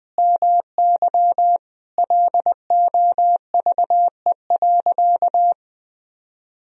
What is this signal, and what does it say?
Morse code, heard as a keyed tone, about 20 wpm, 700 Hz. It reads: MY LOVE.
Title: Sentences from Top 400 Words and Abbreviations 20wpm